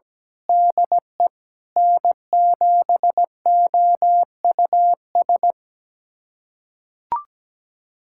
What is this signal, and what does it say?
Morse code, heard as a keyed tone, about 17 wpm, 700 Hz. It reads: DE N7OUS
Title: USA Callsign DE Callsign - 5 Characters - 17wpm